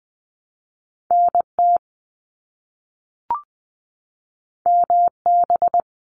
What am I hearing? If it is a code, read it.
Morse code, heard as a keyed tone, about 20 wpm, 700 Hz. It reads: NT MB
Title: Canadian Provinces and Territories 20wpm